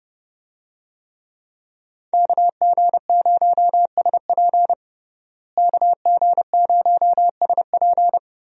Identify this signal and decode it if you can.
Morse code, heard as a keyed tone, about 30 wpm, 700 Hz. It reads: KG0SP KG0SP
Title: USA Call Signs - 5 Characters - Repeated 3x - 30wpm